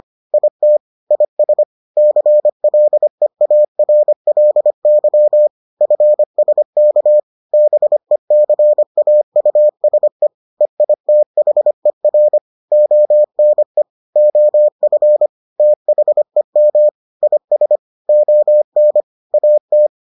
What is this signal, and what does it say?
Morse code, heard as a keyed tone, about 25 wpm, 600 Hz. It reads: IT IS CLEARLY FSK BECAUSE EITHER ONE OF THEM IS ON AT